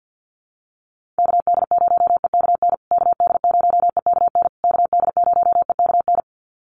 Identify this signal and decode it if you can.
Morse code, heard as a keyed tone, about 48 wpm, 700 Hz. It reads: KD0EKN KD0EKN KD0EKN
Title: USA Call Signs Repeated 3x 50wpm